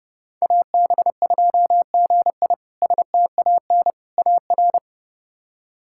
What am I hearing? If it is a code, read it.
Morse code, heard as a keyed tone, about 30 wpm, 700 Hz. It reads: AB2GI STAN AR
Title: K1USN SST QSO Practice - 30wpm